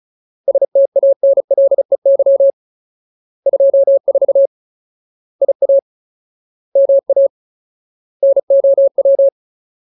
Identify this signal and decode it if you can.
Morse code, heard as a keyed tone, about 36 wpm, 550 Hz. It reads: STANLEY 24 IA MA NOW
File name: Ultimate Word List / 35WPM Ultimate